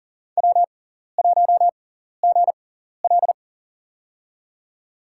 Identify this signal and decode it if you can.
Morse code, heard as a keyed tone, about 40 wpm, spaced as for 14 wpm, 700 Hz. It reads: W1GL
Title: USA Call Signs - 4 Characters - 6x Character Spacing - 40wpm